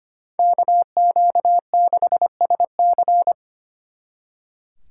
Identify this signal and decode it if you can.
Morse code, heard as a keyed tone, about 32 wpm, 700 Hz. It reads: KQ6SC